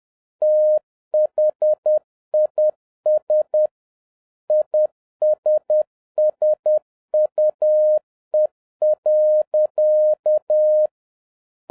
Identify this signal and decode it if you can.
Morse code, heard as a keyed tone, about 10 wpm, 610 Hz. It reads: THIS ISSUE.